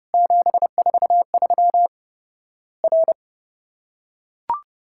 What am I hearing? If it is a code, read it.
Morse code, heard as a keyed tone, about 30 wpm, 700 Hz. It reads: 743 R
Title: Worked All Europe DX Contest - QTC Exchange - 30wpm